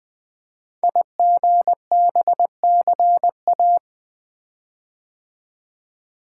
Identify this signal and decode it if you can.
Morse code, heard as a keyed tone, about 20 wpm, 700 Hz. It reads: IGBCA
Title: Code Groups - 5 Letters - 20wpm